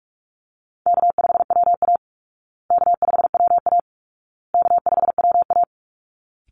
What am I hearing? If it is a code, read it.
Morse code, heard as a keyed tone, about 44 wpm, 700 Hz. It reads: K5WA K5WA K5WA
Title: Easy Call Signs Repeated 3x - 45wpm